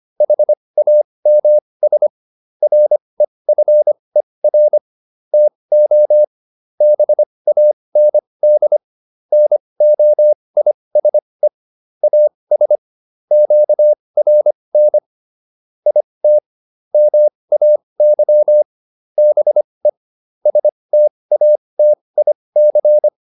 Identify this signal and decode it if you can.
Morse code, heard as a keyed tone, about 25 wpm, spaced as for 20 wpm, 600 Hz. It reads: HAMS REFER TO BAND NOISE AS QRN IT MAY BE STATIC